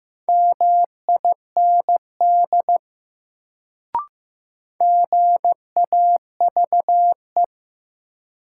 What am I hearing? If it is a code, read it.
Morse code, heard as a keyed tone, about 15 wpm, 700 Hz. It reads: MIND GAVE
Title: Sentences from Top 500 Words - New Word Drill - 15wpm